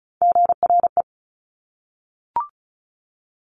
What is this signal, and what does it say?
Morse code, heard as a keyed tone, about 36 wpm, 700 Hz. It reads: GRE
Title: Top Three Letter Combinations in Words 35wpm